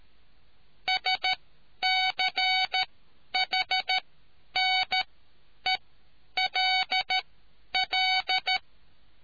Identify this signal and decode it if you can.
Morse code, heard as a keyed tone, about 13 wpm, spaced as for 9 wpm, 750 Hz. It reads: SCHNELL